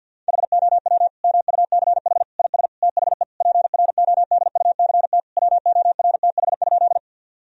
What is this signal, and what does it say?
Morse code, heard as a keyed tone, about 50 wpm, 700 Hz. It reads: HOW MUCH IS THE PRODUCT WORTH?
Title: Sentences from Top 1000 Words - Review Part 2 - 50wpm